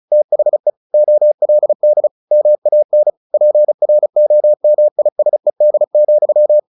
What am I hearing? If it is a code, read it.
Morse code, heard as a keyed tone, about 34 wpm, 600 Hz. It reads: THE OLD MAN PROMISED,
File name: the old man and the sea 2 35wpm